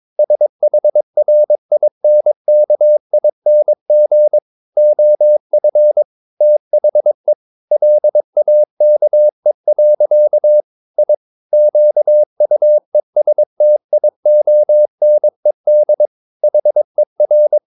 Morse code, heard as a keyed tone, about 22 wpm, 600 Hz. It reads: SHRINKING OF THE LAKE. I QUESTIONED HER
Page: Romanen Never af Ken Follet på engelsk.